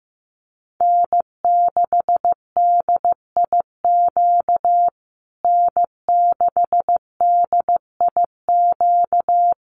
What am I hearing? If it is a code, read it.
Morse code, heard as a keyed tone, about 15 wpm, 700 Hz. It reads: N6DIQ N6DIQ
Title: USA Call Signs Repeated 3x 15wpm